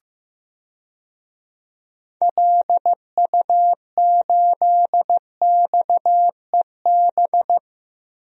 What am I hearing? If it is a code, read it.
Morse code, heard as a keyed tone, about 16 wpm, 700 Hz. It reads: LU8XEB
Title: DXCC-100 Call Signs - 6 Characters - Repeated 2x - 15wpm